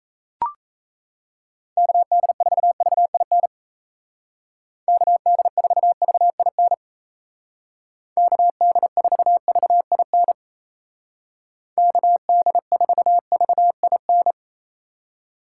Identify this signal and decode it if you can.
Morse code, heard as a keyed tone, about 35 wpm, 700 Hz. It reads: KD4VIN KD4VIN KD4VIN KD4VIN
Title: Speed Racing - USA 2x3 Call Signs 28wpm